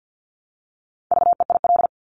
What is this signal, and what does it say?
Morse code, heard as a keyed tone, about 50 wpm, 700 Hz. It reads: VEIL
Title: Multi-letter Phonemes Repeated 3x 50wpm